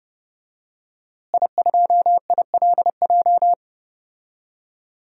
Morse code, heard as a keyed tone, about 30 wpm, 700 Hz. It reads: I2ILJ
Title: DXCC-100 Call Signs - 5 Characters - Repeated 2x - 30wpm